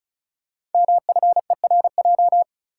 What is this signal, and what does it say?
Morse code, heard as a keyed tone, about 36 wpm, 700 Hz. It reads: MFERJ
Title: Code Groups - 5 Letters - 35wpm